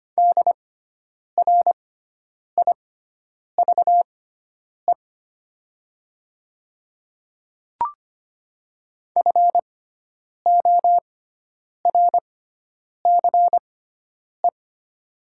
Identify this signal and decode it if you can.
Morse code, heard as a keyed tone, about 25 wpm, spaced as for 9 wpm, 700 Hz. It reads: DRIVE FORCE
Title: 6x Character Spacing - Top 500 Words - 25wpm